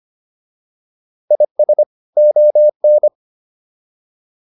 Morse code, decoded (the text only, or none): IS ON